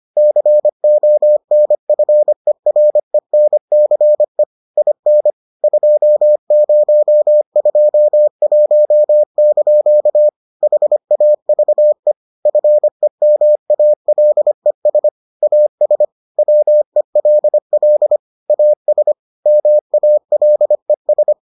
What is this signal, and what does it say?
Morse code, heard as a keyed tone, about 25 wpm, 600 Hz. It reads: CONFERENCE IN 2021) HAVE FEMALES AS WELL AS MALES